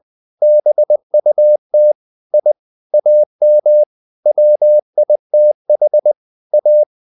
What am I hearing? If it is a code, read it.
Morse code, heard as a keyed tone, about 20 wpm, 600 Hz. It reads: BUT I AM WITH A
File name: the old man and the sea / the old man and the sea part 1 20wpm